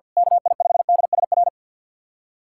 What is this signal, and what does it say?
Morse code, heard as a keyed tone, about 45 wpm, 700 Hz. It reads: KI5DSR
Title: USA Callsign DE Callsign 50wpm